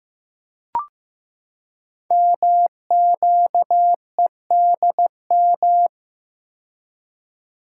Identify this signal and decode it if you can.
Morse code, heard as a keyed tone, about 15 wpm, 700 Hz. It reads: MQEDM
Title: Code Groups - 5 Letter:Numbers - 15wpm